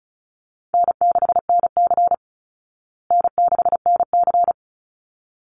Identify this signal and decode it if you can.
Morse code, heard as a keyed tone, about 34 wpm, 700 Hz. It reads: N6NC N6NC
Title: Call Signs Repeated 3x - 35wpm